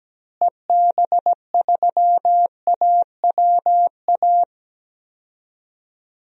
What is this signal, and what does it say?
Morse code, heard as a keyed tone, about 17 wpm, 700 Hz. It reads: EB3AWA
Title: DXCC-100 Call Signs - 6 Characters - Repeated 2x - 17wpm